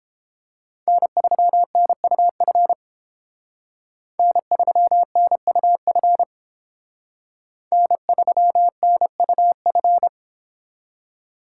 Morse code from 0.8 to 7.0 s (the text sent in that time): N3NUF N3NUF